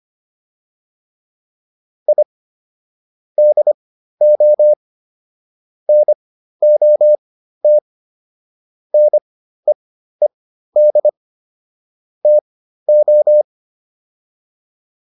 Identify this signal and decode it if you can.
Morse code, heard as a keyed tone, about 25 wpm, spaced as for 13 wpm, 600 Hz. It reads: I DO NOT NEED TO